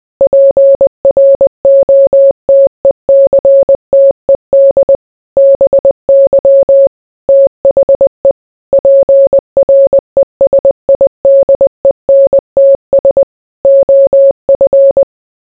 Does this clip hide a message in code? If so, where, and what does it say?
Yes: Morse code from 0.2 to 15.0 s, heard as a keyed tone, about 20 wpm, 550 Hz. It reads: PROTECTED BY THE PRESIDENTS OF